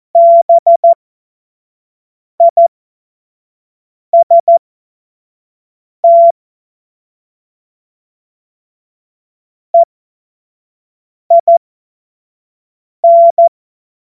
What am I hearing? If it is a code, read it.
Morse code, heard as a keyed tone, about 14 wpm, spaced as for 5 wpm, 680 Hz. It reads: BIST EIN
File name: Morsekurs / Witze / Witz 21. September 2021, 25 F70